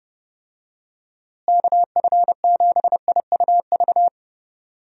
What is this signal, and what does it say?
Morse code, heard as a keyed tone, about 30 wpm, 700 Hz. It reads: KF7IUV
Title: USA Call Signs - 6 Characters - Repeated 2x - 30wpm